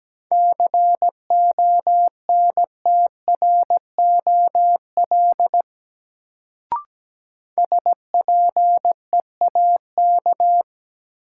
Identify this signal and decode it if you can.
Morse code, heard as a keyed tone, about 17 wpm, 700 Hz. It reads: CONTROL SPEAK
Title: Sentences from Top 1000 Words - Review Part 2 - New Word Drill - 17wpm